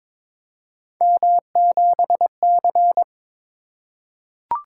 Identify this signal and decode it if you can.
Morse code, heard as a keyed tone, about 22 wpm, 700 Hz. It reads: M7C
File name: practice / 22wpm / Three Letter-Number Combinations - 22wpm